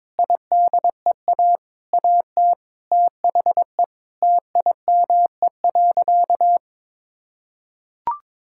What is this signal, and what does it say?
Morse code, heard as a keyed tone, about 22 wpm, 700 Hz. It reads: IDEA AT THE TIME.
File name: Sentences with 9-Word Phrases - 22wpm